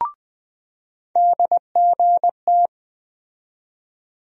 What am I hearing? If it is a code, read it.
Morse code, heard as a keyed tone, about 20 wpm, 700 Hz. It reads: DGT
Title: USA Call Sign Postfixes - 3 Letters 20wpm